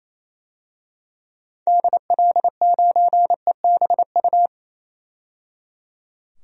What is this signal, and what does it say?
Morse code, heard as a keyed tone, about 28 wpm, 700 Hz. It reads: DL9EBU
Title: DXCC-100 Call Signs - 6 Characters - 28wpm